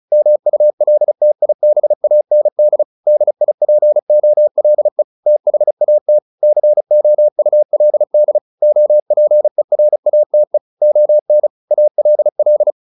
Morse code, heard as a keyed tone, about 34 wpm, 600 Hz. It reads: MULTIBAND DIPOLE THAT COULD OPERATE ON ALL